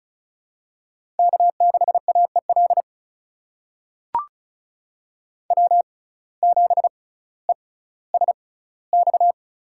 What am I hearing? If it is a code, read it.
Morse code, heard as a keyed tone, about 36 wpm, spaced as for 18 wpm, 700 Hz. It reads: K6AEL W7ESX
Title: USA Call Signs - 5 Characters - 6x Character Spacing - 35wpm